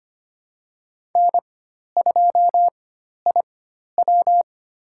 Morse code, heard as a keyed tone, about 25 wpm, spaced as for 12 wpm, 700 Hz. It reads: N2IW
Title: USA Call Signs - 4 Characters - 4x Character Spacing - 25wpm